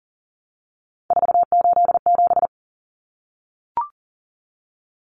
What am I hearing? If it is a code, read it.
Morse code, heard as a keyed tone, about 40 wpm, 700 Hz. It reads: 487
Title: Numbers - 3-Digit 40wpm